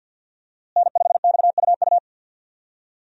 Morse code, heard as a keyed tone, about 50 wpm, 700 Hz. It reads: N5XUU
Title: USA Call Signs - 5 Characters - Repeated 2x - 50wpm